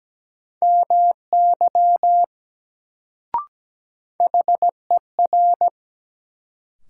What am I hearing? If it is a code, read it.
Morse code, heard as a keyed tone, about 17 wpm, 700 Hz. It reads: MY HER
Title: Sets of 2 Words - New Word Drill - 17wpm